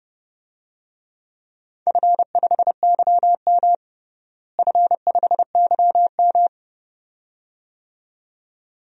Morse code, heard as a keyed tone, about 32 wpm, 700 Hz. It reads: F5YM F5YM
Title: DXCC-100 Call Signs - 4 Characters - Repeated 2x - 30wpm